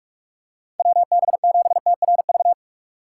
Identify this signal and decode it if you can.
Morse code, heard as a keyed tone, about 44 wpm, 700 Hz. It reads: WB7TRV
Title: USA Call Signs - 6x Character Spacing - 45wpm